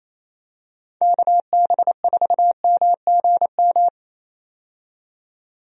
Morse code, heard as a keyed tone, about 28 wpm, 700 Hz. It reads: KB4MGM
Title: USA Call Signs - 6 Characters - 28wpm